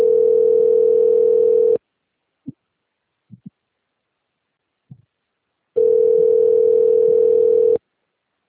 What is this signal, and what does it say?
Ringback tone (North America), 440 + 480 Hz.
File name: rg-1600-unknown- - - .55141